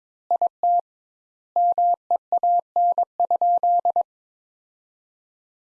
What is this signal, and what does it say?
Morse code, heard as a keyed tone, about 22 wpm, 700 Hz. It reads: IT MEAN?
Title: Sentences Top 200 Words - 22wpm